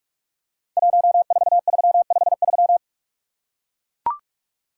Morse code, heard as a keyed tone, about 44 wpm, 700 Hz. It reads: 14353
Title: Code Groups - 5 Letter:Numbers - 45wpm